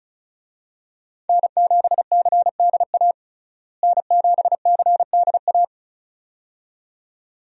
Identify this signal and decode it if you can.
Morse code, heard as a keyed tone, about 36 wpm, 700 Hz. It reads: N7CDA N7CDA
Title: USA Call Signs - 5 Characters - Repeated 2x - 35wpm